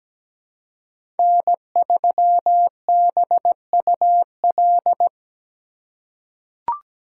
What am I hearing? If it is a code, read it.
Morse code, heard as a keyed tone, about 17 wpm, 700 Hz. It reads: N3BUL
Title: USA Call Signs - 2x Character Spacing - 17wpm